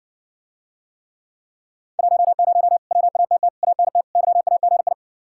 Morse code, heard as a keyed tone, about 60 wpm, 700 Hz. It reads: 10 WATT ANT YAGI